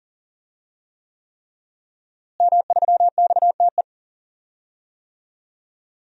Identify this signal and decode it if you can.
Morse code, heard as a keyed tone, about 40 wpm, 700 Hz. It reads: M3XTE
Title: DXCC-100 Call Signs - 5 Characters - Repeated 3x - 40wpm